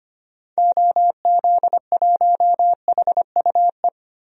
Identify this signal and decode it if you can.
Morse code, heard as a keyed tone, about 25 wpm, 700 Hz. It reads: OZ1HUE
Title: DXCC-100 Call Signs - 6 Characters - 25wpm